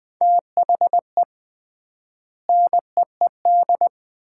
Morse code, heard as a keyed tone, about 20 wpm, 700 Hz. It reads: THE NEED
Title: 3x Word-Spacing - Sets of 3 Words - 20wpm